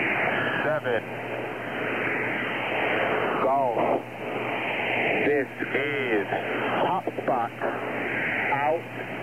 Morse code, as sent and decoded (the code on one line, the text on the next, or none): none